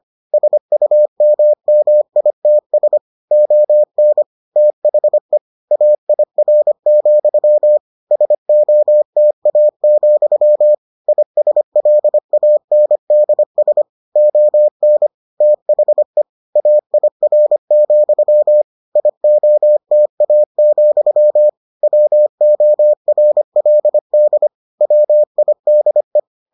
Morse code, heard as a keyed tone, about 25 wpm, 600 Hz. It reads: SUMMITS ON THE AIR, SOTA, ISLANDS ON THE AIR, IOTA, WORLD WIDE